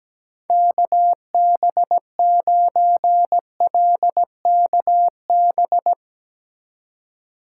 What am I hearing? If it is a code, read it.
Morse code, heard as a keyed tone, about 17 wpm, 700 Hz. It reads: KB9LKB
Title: USA Call Signs - 6 Characters - 2x Character Spacing - 17wpm